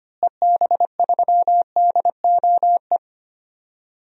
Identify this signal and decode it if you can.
Morse code, heard as a keyed tone, about 25 wpm, 700 Hz. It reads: EB3DOE
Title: DXCC-100 Call Signs - 6 Characters - Repeated 3x - 25wpm